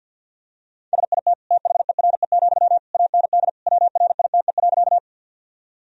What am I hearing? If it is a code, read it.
Morse code, heard as a keyed tone, about 48 wpm, 700 Hz. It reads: SIT THERE, AND WRITE.
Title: Sentences from Top 1000 Words - Review Part 1 - 50wpm